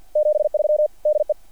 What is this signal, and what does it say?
Morse code, heard as a keyed tone, about 48 wpm, 600 Hz. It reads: 73 DE
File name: archived / CRSnet